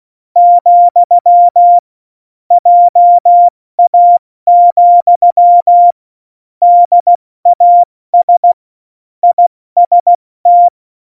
Morse code, heard as a keyed tone, about 16 wpm, 700 Hz. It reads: , JA, DAS IST
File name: Morsekurs / Witze / Witz 16. Juli 2019, 70 F80